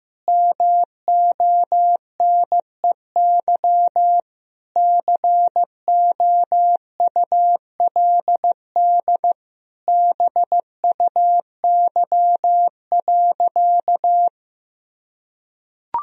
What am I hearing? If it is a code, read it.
Morse code, heard as a keyed tone, about 16 wpm, 700 Hz. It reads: MONEY COULD BUY.
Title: Sentences from Top 1000 Words - Review Part 4 - 15wpm